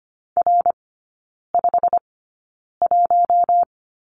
Morse code, heard as a keyed tone, about 25 wpm, spaced as for 9 wpm, 690 Hz. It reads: R51
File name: Lesson 15e - Number 5 - Three Characters 25wpm